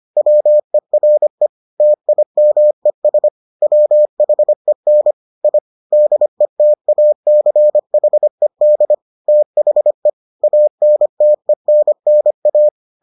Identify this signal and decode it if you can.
Morse code, heard as a keyed tone, about 25 wpm, 600 Hz. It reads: WERE TIMES WHEN I DETACHED THE ANTENNA